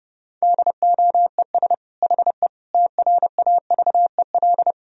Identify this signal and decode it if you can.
Morse code, heard as a keyed tone, about 30 wpm, 700 Hz. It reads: DOES HE TRAVEL
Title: Sentences from Top 1000 Words - Review Part 2 - 30wpm